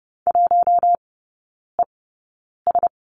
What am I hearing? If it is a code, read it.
Morse code, heard as a keyed tone, about 30 wpm, spaced as for 9 wpm, 700 Hz. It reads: 1ES
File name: Lesson 16e - Letter c - Three Characters 30wpm